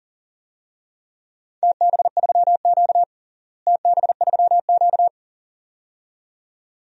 Morse code, heard as a keyed tone, about 40 wpm, 700 Hz. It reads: TB3Q TB3Q
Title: DXCC-100 Call Signs - 4 Characters - Repeated 2x - 40wpm